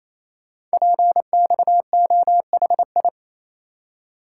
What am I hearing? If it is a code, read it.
Morse code, heard as a keyed tone, about 29 wpm, 700 Hz. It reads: PXOHI
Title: Code Groups - 5 Letters - 28wpm